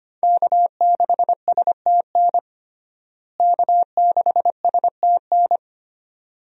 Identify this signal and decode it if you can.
Morse code, heard as a keyed tone, about 25 wpm, 700 Hz. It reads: K6STN K6STN
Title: USA Call Signs - 5 Characters - Repeated 2x - 25wpm